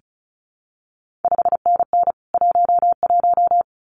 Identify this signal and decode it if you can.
Morse code, heard as a keyed tone, about 36 wpm, 700 Hz. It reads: 5NN 11
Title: CQ WW DX Contest Exchange Encore - 35wpm